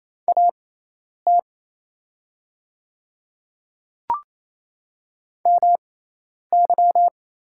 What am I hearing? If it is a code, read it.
Morse code, heard as a keyed tone, about 27 wpm, spaced as for 10 wpm, 690 Hz. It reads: AT MY
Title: 6x Character Spacing - Top 100 Words - 28wpm